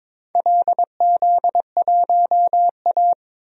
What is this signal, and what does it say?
Morse code, heard as a keyed tone, about 22 wpm, 700 Hz. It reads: LZ1A
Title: DXCC-100 Call Signs - 4 Characters - 22wpm